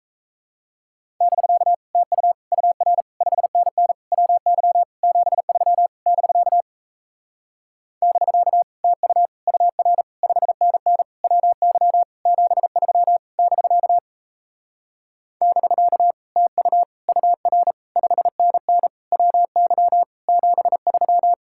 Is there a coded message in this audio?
Yes: Morse code from 1.2 to 21.3 s, heard as a keyed tone, about 38 wpm, 700 Hz. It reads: <BK> TU UR 5NN WY 73 <BK> <BK> TU UR 5NN WY 73 <BK> <BK> TU UR 5NN WY 73